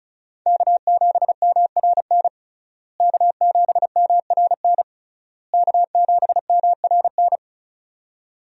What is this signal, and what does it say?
Morse code, heard as a keyed tone, about 34 wpm, 700 Hz. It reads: K7MRN K7MRN K7MRN
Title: USA Call Signs - 5 Characters - Repeated 3x - 35wpm